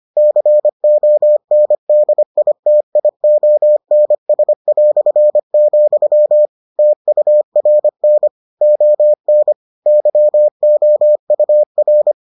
Morse code, heard as a keyed tone, about 25 wpm, 600 Hz. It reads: CONDITIONS", TURN ON YOUR